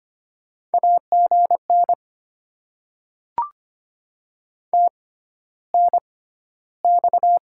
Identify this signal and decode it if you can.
Morse code, heard as a keyed tone, about 25 wpm, spaced as for 13 wpm, 700 Hz. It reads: AGN TNX
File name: Top 90 QSO Elements - 6x Character Spacing - 25wpm